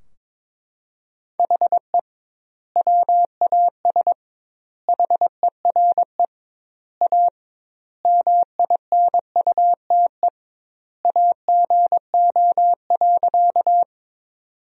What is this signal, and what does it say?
Morse code, heard as a keyed tone, about 22 wpm, 700 Hz. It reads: HE WAS HERE A MINUTE AGO.
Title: 2x Word-Spacing - Sentences from Top 500 Words - 22wpm